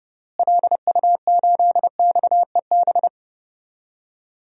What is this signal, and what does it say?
Morse code, heard as a keyed tone, about 29 wpm, 700 Hz. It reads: LU8XEB
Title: DXCC-100 Call Signs - 6 Characters - Repeated 2x - 30wpm